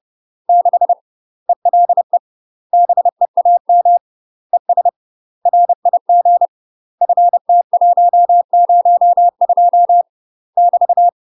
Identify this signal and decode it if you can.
Morse code, heard as a keyed tone, about 30 wpm, 700 Hz. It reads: 6 ELE BEAM ES RIG FT102 <BT>
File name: UK MASTER QSOs TRAINING / 30WPM / UK MASTER QSOs 0020